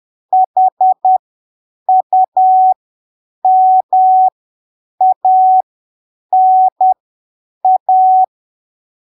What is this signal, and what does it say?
Morse code, heard as a keyed tone, about 10 wpm, spaced as for 7 wpm, 750 Hz. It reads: HUMANA
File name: arquivos / texto / texto 25 10ppm